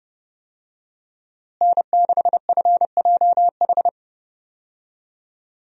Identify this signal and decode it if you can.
Morse code, heard as a keyed tone, about 30 wpm, 700 Hz. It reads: N6FJH
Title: USA Call Signs - 5 Characters - Repeated 3x - 30wpm